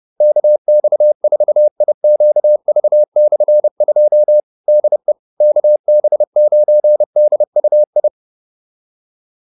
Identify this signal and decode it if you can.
Morse code, heard as a keyed tone, about 30 wpm, 600 Hz. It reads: KX4IQV/2 DE KB9DUI